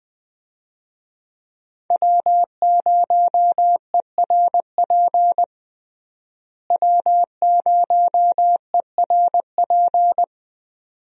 Morse code, heard as a keyed tone, about 21 wpm, 700 Hz. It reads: W0ERP W0ERP
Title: USA Call Signs - 5 Characters - Repeated 2x - 20wpm